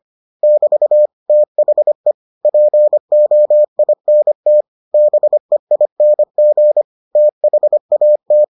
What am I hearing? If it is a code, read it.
Morse code, heard as a keyed tone, about 25 wpm, 600 Hz. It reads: <BT> THE POINT BEING THAT